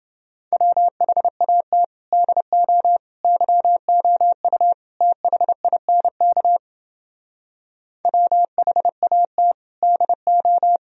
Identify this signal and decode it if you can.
Morse code, heard as a keyed tone, about 29 wpm, 700 Hz. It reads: WHAT DO YOU THINK WHAT DO